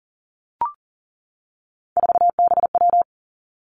Morse code, heard as a keyed tone, about 39 wpm, 700 Hz. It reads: 4BW
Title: Three Challenging Letter-Number Combinations 40wpm